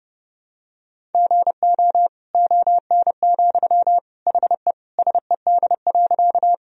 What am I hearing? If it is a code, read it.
Morse code, heard as a keyed tone, about 30 wpm, 700 Hz. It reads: GO ON, HE SED.
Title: Sentences from Top 100 Words and Abbreviations 30wpm